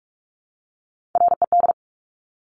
Morse code, heard as a keyed tone, about 46 wpm, 690 Hz. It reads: RED
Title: Top Three Letter Combinations in Words 45wpm